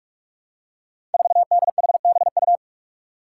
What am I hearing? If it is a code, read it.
Morse code, heard as a keyed tone, about 46 wpm, 700 Hz. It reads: 4DHBU
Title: Five Challenging Letter-Number Combinations 45wpm